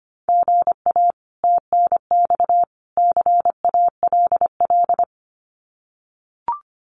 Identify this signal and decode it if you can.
Morse code, heard as a keyed tone, about 25 wpm, 700 Hz. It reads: GA TNX CALL
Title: Common QSO Phrases 25wpm